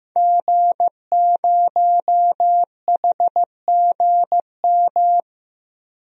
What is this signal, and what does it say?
Morse code, heard as a keyed tone, about 15 wpm, 700 Hz. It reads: G0HGM
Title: DXCC-100 Call Signs - 5 Characters - 15wpm